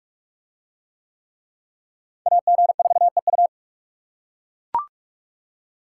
Morse code, heard as a keyed tone, about 40 wpm, 700 Hz. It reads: AG4EU